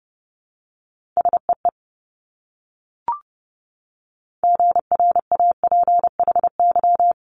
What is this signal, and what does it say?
Morse code, heard as a keyed tone, about 30 wpm, 700 Hz. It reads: SEE GRAPHY
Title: Multi-letter Phonemes Repeated 3x 30wpm